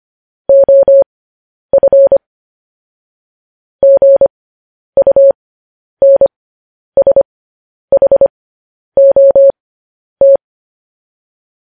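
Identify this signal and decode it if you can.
Morse code, heard as a keyed tone, about 25 wpm, spaced as for 10 wpm, 550 Hz. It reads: OF GUNSHOT
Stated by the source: Morse code news headlines broadcast at 10 wpm.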